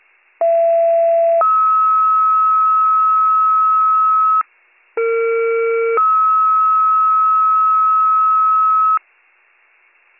Two-tone page, heard as a one-second tone, then a three-second tone, about 670 then 1290 Hz, about 460 then 1290 Hz. It is heard from 0.4 to 9.0 s.